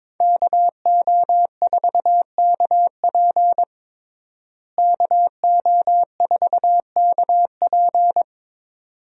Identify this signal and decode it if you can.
Morse code, heard as a keyed tone, about 22 wpm, 700 Hz. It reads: KO4KP KO4KP
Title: USA Call Signs - 5 Characters - Repeated 2x - 22wpm